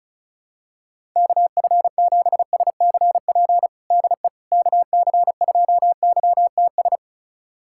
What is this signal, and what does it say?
Morse code, heard as a keyed tone, about 34 wpm, 700 Hz. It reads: KF7SCP DE KC2YTS
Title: USA Callsign DE Callsign - 6 Characters - 35wpm